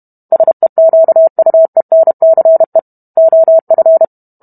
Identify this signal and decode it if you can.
Morse code, heard as a keyed tone, about 32 wpm, 650 Hz. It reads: SEQUENCE OF